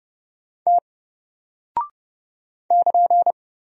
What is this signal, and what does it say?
Morse code, heard as a keyed tone, about 30 wpm, 700 Hz. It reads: T <KN>
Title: Single Character (Letter:Number:Punctuation:Prosign) 30wpm